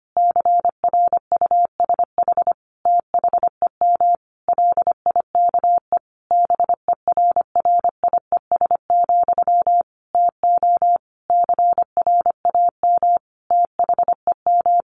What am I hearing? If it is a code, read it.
Morse code, heard as a keyed tone, about 25 wpm, 700 Hz. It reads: CRUSH THEM LIKE BERRIES, TO CRAM THEM
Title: A Sound Of Thunder 25wpm